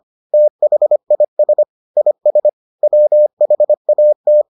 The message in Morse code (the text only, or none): THIS IS WHAT